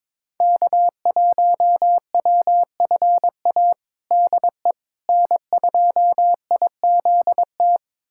Morse code, heard as a keyed tone, about 22 wpm, 700 Hz. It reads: K1WFA DE N2IZT
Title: USA Callsign DE Callsign - 5 Characters - 22wpm